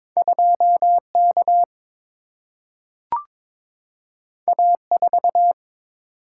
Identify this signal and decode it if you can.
Morse code, heard as a keyed tone, about 22 wpm, 700 Hz. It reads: 2 K A 4
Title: Single-Letter Single-Number 22wpm